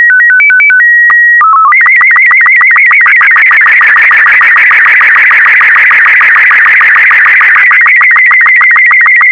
SSTV (slow-scan television), Robot 36.